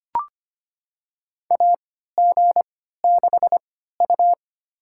Morse code, heard as a keyed tone, about 25 wpm, spaced as for 14 wpm, 700 Hz. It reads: AG6U
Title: USA Call Signs - 4 Characters - 3x Character Spacing - 25wpm